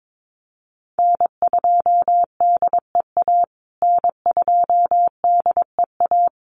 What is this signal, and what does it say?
Morse code, heard as a keyed tone, about 22 wpm, 700 Hz. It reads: N2DEA N2DEA
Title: USA Call Signs Repeated 3x 22wpm